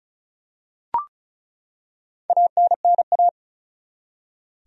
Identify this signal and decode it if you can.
Morse code, heard as a keyed tone, about 34 wpm, 700 Hz. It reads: ANNA
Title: USA Common Names - Max 5 Letters - 35wpm